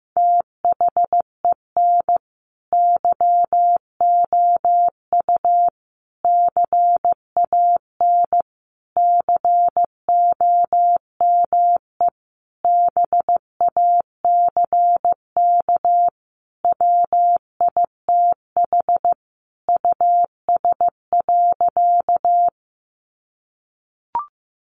Morse code, heard as a keyed tone, about 15 wpm, 700 Hz. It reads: THEN YOU CAN COME BACK WITH US.
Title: A Sound Of Thunder 15wpm